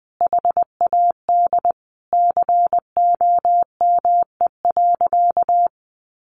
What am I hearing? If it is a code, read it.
Morse code, heard as a keyed tone, about 20 wpm, 700 Hz. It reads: HAD COME.
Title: Sentences from Top 1100 Words 20wpm